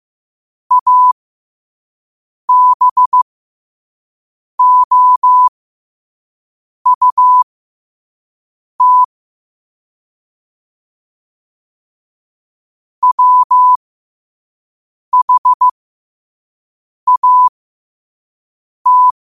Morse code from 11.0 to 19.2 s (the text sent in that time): WHAT